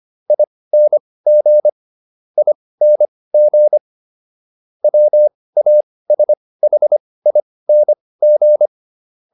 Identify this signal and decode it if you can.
Morse code, heard as a keyed tone, about 25 wpm, spaced as for 18 wpm, 600 Hz. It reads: ING ING WASHING